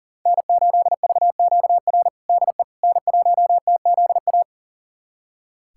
Morse code, heard as a keyed tone, about 40 wpm, 700 Hz. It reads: N8VQR DE N1TZA
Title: USA Callsign DE Callsign - 5 Characters - 40wpm